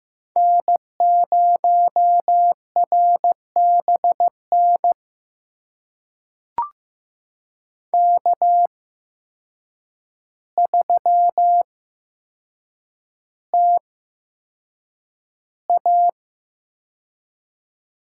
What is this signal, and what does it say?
Morse code, heard as a keyed tone, about 15 wpm, spaced as for 7 wpm, 700 Hz. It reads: N0RBN K3TA
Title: USA Call Signs - 8x Character Spacing - 15wpm